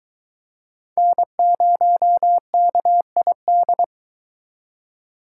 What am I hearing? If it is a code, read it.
Morse code, heard as a keyed tone, about 23 wpm, 700 Hz. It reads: N0KID